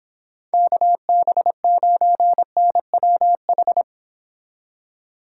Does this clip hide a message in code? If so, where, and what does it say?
Yes: Morse code from 0.5 to 3.8 s, heard as a keyed tone, about 26 wpm, 700 Hz. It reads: KB9NWH